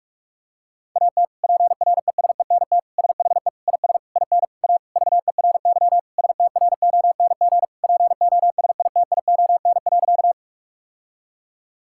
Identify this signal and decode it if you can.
Morse code, heard as a keyed tone, about 46 wpm, 700 Hz. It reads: AT PRESENT SHE IS IN A VERY STRONG POSITION.
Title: Sentences from Top 1000 Words - Review Part 5 - 45wpm